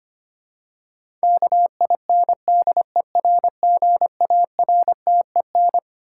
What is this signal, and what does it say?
Morse code, heard as a keyed tone, about 25 wpm, 700 Hz. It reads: KINDERGARTEN
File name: Twelve Letter Words - 25wpm